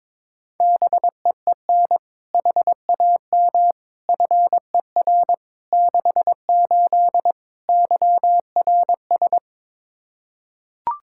Morse code, heard as a keyed tone, about 21 wpm, 700 Hz. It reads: BEEN HAM FER 68 YRS
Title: Practice QSO - Exchange 3 - More Info - 22wpm